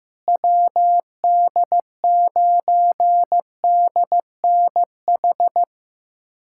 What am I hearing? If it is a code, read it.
Morse code, heard as a keyed tone, about 15 wpm, 700 Hz. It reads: WD9DNH
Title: USA Callsign DE Callsign - 6 Characters - 15wpm